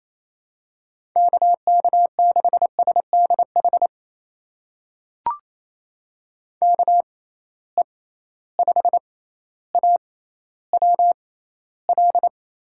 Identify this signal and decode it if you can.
Morse code, heard as a keyed tone, about 28 wpm, spaced as for 14 wpm, 700 Hz. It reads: KK6SDH KE5AWL
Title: USA Call Signs - 6 Characters - 6x Character Spacing - 28wpm